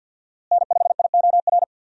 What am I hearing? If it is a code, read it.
Morse code, heard as a keyed tone, about 50 wpm, 700 Hz. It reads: N5IOR
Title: USA Call Signs - 5 Characters - Repeated 3x - 50wpm